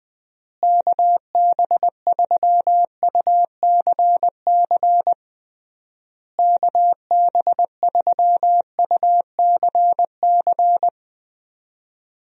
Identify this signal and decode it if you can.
Morse code, heard as a keyed tone, about 20 wpm, 700 Hz. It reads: KB3UCC KB3UCC
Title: USA Call Signs - 6 Characters - Repeated 3x - 20wpm